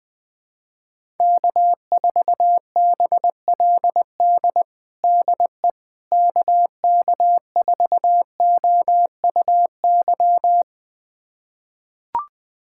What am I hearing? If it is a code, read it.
Morse code, heard as a keyed tone, about 20 wpm, 700 Hz. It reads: K4BLD DE KK4OUY
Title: USA Callsign DE Callsign 20wpm